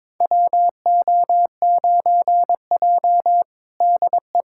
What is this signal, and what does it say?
Morse code, heard as a keyed tone, about 21 wpm, 700 Hz. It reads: WO9J DE
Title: USA Callsign DE Callsign - 4 Characters - 22wpm